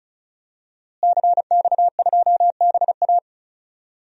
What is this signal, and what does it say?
Morse code, heard as a keyed tone, about 34 wpm, 700 Hz. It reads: CX2BA
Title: DXCC-100 Call Signs - 5 Characters - Repeated 2x - 35wpm